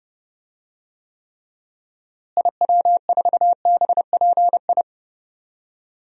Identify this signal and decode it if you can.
Morse code, heard as a keyed tone, about 31 wpm, 700 Hz. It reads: IW4BPI
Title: DXCC-100 Call Signs - 6 Characters - Repeated 2x - 30wpm